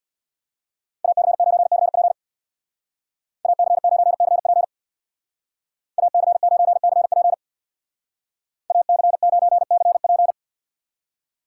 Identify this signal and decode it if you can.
Morse code, heard as a keyed tone, about 64 wpm, 700 Hz. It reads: AC9CP AC9CP AC9CP AC9CP